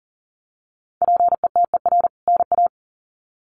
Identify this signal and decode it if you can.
Morse code, heard as a keyed tone, about 40 wpm, 700 Hz. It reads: PETER NA
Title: CWT Exchange 40wpm